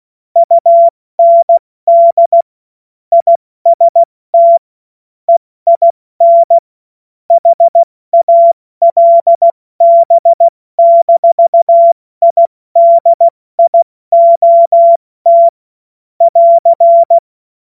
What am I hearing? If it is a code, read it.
Morse code, heard as a keyed tone, about 16 wpm, 680 Hz. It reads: UND IST EIN HALB-IDIOT <AR>
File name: Morsekurs / Witze / Witz 14. September 2021, 70 F80